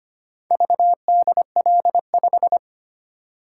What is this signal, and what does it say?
Morse code, heard as a keyed tone, about 25 wpm, 700 Hz. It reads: VDL5
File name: practice / 25wpm / Four Letter-Number Combinations - 25wpm